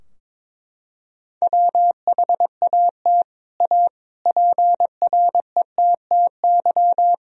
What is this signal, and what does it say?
Morse code, heard as a keyed tone, about 22 wpm, 700 Hz. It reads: WHAT A PRETTY
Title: Sentences from Top 1000 Words - Review Part 3 - 22wpm